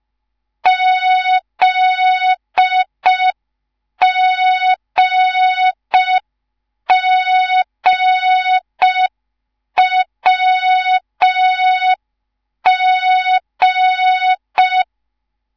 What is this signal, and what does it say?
Morse code, heard as a keyed tone, about 5 wpm, 3700 Hz. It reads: ZGGWG
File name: Morze-HA2MN / hang / 10GWZ HA2MN